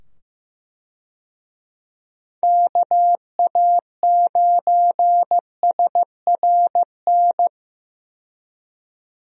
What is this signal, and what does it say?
Morse code, heard as a keyed tone, about 15 wpm, 700 Hz. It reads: KA9SRN
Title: USA Call Signs - 6 Characters - 15wpm